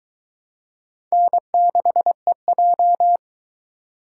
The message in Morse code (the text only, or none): N6EJ